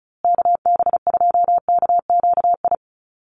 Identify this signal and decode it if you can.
Morse code, heard as a keyed tone, about 34 wpm, 700 Hz. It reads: KB2KQI
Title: USA Call Signs Repeated 3x 35wpm